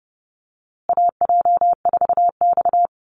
Morse code, heard as a keyed tone, about 30 wpm, 700 Hz. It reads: AJ4X
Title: USA Call Signs Repeated 2x 30wpm